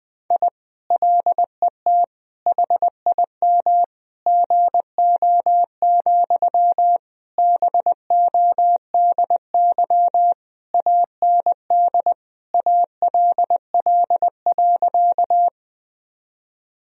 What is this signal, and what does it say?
Morse code, heard as a keyed tone, about 20 wpm, 700 Hz. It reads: I LET HIM GO, BODY AND ALL.
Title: Sentences from Top 1000 Words - Review Part 2 - 20wpm